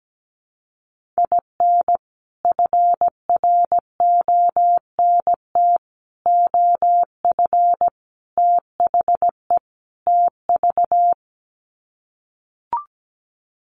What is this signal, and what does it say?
Morse code, heard as a keyed tone, about 17 wpm, 700 Hz. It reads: IN FRONT OF THE TV
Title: Sets of 5 Words Encore2 17wpm